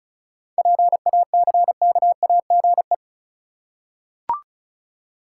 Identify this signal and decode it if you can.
Morse code, heard as a keyed tone, about 36 wpm, 700 Hz. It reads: PACKAGE
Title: Seven Letter Words - 35wpm